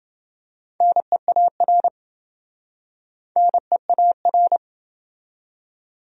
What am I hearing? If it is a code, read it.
Morse code, heard as a keyed tone, about 28 wpm, 700 Hz. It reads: NEAR NEAR